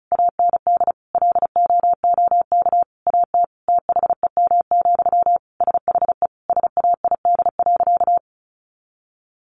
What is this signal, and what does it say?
Morse code, heard as a keyed tone, about 34 wpm, 700 Hz. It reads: AND LOOK AT THEM, SHE SAID.
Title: Sentences from Top 1400 Words 35wpm